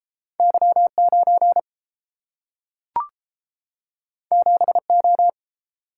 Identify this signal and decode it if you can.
Morse code, heard as a keyed tone, about 30 wpm, 700 Hz. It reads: Y9 7O